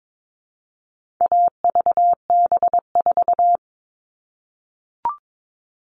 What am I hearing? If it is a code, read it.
Morse code, heard as a keyed tone, about 22 wpm, 700 Hz. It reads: AVB4
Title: All Cut Numbers - 4-Digit 22wpm